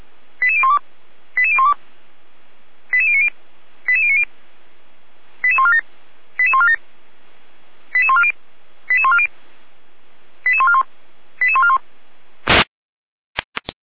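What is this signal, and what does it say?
Five-tone selective call (ZVEI), heard as quick runs of five tones, tones of about 70 ms. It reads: 88012 88012 88090 88090 88147 88147 88140 88140 88141 88141
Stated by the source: ZVEI five-tone selective call received via SDR, from feuerwehr-orlamuende (Orlamünde fire brigade).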